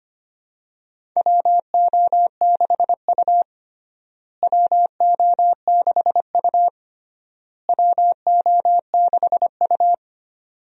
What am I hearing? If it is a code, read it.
Morse code, heard as a keyed tone, about 25 wpm, 700 Hz. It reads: WO6U WO6U WO6U
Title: USA Call Signs - 4 Characters - Repeated 3x - 25wpm